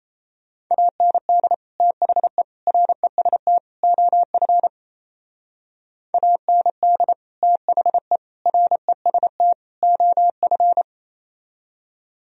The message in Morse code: AND THE REST OF AND THE REST OF

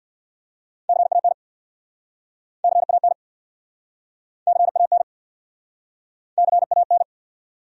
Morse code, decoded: CAN CAN CAN CAN

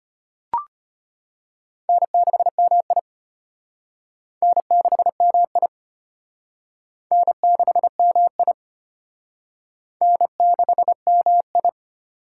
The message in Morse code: N6MI N6MI N6MI N6MI

